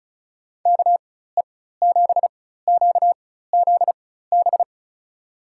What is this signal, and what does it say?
Morse code, heard as a keyed tone, about 34 wpm, spaced as for 16 wpm, 700 Hz. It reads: KE7QZB
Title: USA Call Signs - 6 Characters - 4x Character Spacing - 35wpm